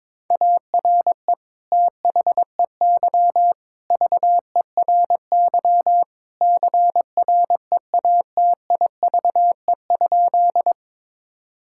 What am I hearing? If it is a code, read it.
Morse code, heard as a keyed tone, about 22 wpm, 700 Hz. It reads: ARE THEY VERY CREATIVE?
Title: Sentences from Top 1000 Words - Review Part 5 - 22wpm